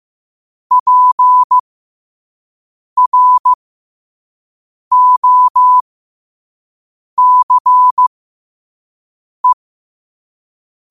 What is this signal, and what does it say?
Morse code, heard as a keyed tone, about 15 wpm, spaced as for 5 wpm, 1000 Hz. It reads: PROCE
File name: qotdfiles / 5wpm